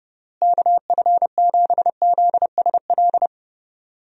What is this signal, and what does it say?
Morse code, heard as a keyed tone, about 30 wpm, 700 Hz. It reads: KF7ZSL
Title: USA Call Signs - 6 Characters - Repeated 2x - 30wpm